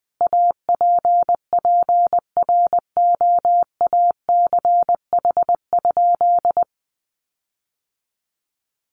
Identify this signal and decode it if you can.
Morse code, heard as a keyed tone, about 20 wpm, 700 Hz. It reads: APPROACH?
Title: Sentences from Top 1500 Words 20wpm